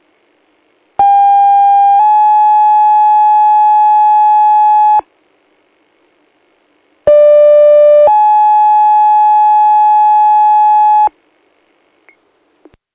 Two-tone page, heard as a one-second tone, then a three-second tone, about 800 then 850 Hz, about 590 then 850 Hz. It is heard from 1.0 to 11.1 s.